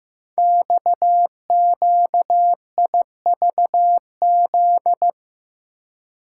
Morse code, heard as a keyed tone, about 15 wpm, 700 Hz. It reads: XQIVZ
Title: Code Groups - 5 Letters - 15wpm